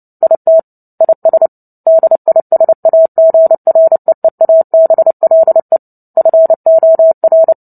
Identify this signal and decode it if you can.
Morse code, heard as a keyed tone, about 30 wpm, 650 Hz. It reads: IT IS DISAGREEABLE FOR
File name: Book Two / 03-Chapter 3 - 29 WPM